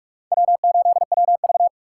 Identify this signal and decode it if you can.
Morse code, heard as a keyed tone, about 45 wpm, 700 Hz. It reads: W8WV
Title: USA Call Signs - 4 Characters - 2x Character Spacing - 45wpm